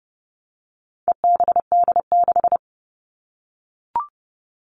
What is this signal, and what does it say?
Morse code, heard as a keyed tone, about 30 wpm, 700 Hz. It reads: EBD6
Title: All Cut Numbers - 4-Digit 30wpm